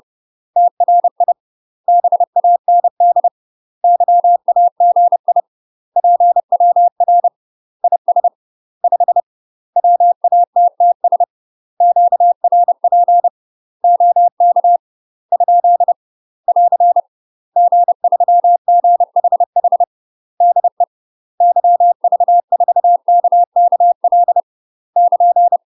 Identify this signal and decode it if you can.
Morse code, heard as a keyed tone, about 30 wpm, 700 Hz. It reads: TRI BAND YAGI PWR IS 5 WATTS QRP OK ? <AR> G3GHH DE YV4KKL <KN>
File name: UK MASTER QSOs TRAINING / 30WPM / UK MASTER QSOs 0034